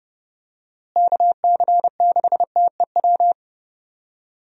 Morse code, heard as a keyed tone, about 30 wpm, 700 Hz. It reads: KC6TEW
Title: USA Call Signs - 6 Characters - Repeated 3x - 30wpm